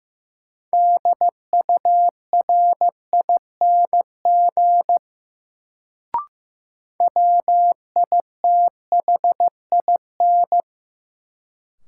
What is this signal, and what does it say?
Morse code, heard as a keyed tone, about 15 wpm, 700 Hz. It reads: DURING WITHIN
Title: Sentences from Top 700 Words - New Word Drill - 15wpm